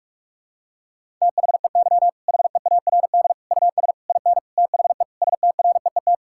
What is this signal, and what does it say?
Morse code, heard as a keyed tone, about 41 wpm, 700 Hz. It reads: THEY HEARD US IN THE STREET
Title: Sentences from Top 1000 Words - Review Part 5 - 45wpm